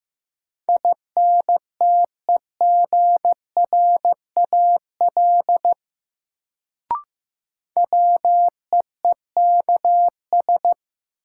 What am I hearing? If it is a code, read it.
Morse code, heard as a keyed tone, about 15 wpm, 700 Hz. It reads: INTEGRAL WEEKS
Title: Sets of 5 Words - New Word Drill - 15wpm